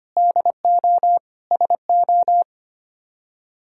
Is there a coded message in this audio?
Yes: Morse code from 0.2 to 2.4 s, heard as a keyed tone, about 33 wpm, 700 Hz. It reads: DO SO